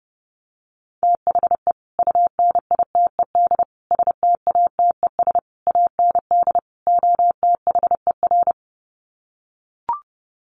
Morse code, heard as a keyed tone, about 30 wpm, 700 Hz. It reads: THE UNITED STATES AND OTHER
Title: Sets of 5 Words Encore 30wpm